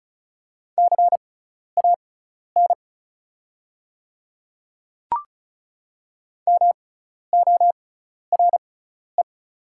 Morse code, heard as a keyed tone, about 36 wpm, spaced as for 12 wpm, 690 Hz. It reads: CAN MORE
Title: 6x Character Spacing - Top 100 Words - 35wpm